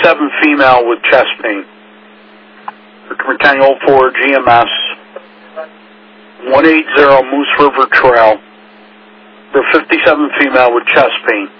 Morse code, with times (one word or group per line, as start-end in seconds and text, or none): none